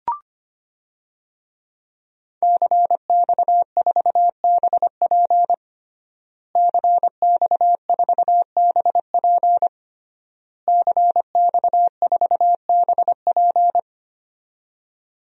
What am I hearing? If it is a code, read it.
Morse code, heard as a keyed tone, about 25 wpm, 700 Hz. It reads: CX4BP CX4BP CX4BP
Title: DXCC-100 Call Signs - 5 Characters - Repeated 3x - 25wpm